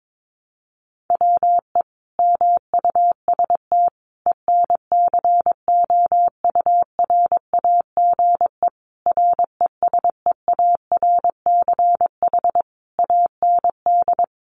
Morse code, heard as a keyed tone, about 22 wpm, 700 Hz. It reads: WE MUST ENCOURAGE RESEARCH AND
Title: Sentences from Top 1400 Words 22wpm